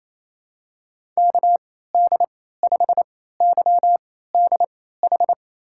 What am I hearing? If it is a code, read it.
Morse code, heard as a keyed tone, about 28 wpm, spaced as for 16 wpm, 700 Hz. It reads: KD5YDH
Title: USA Call Signs - 6 Characters - 3x Character Spacing - 28wpm